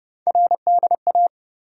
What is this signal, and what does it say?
Morse code, heard as a keyed tone, about 30 wpm, 700 Hz. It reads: RDA
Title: USA Call Sign Postfixes - 3 Letters 30wpm